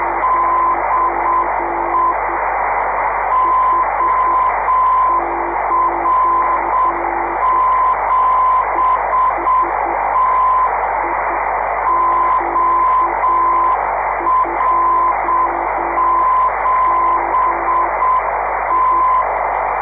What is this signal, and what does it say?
Morse code, heard as a keyed tone, about 7 wpm, 1020 Hz, with radio noise. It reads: DE OR7T OR7T
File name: 500khz / 30 december 2011 / 9h9es or7t-iq2mi rst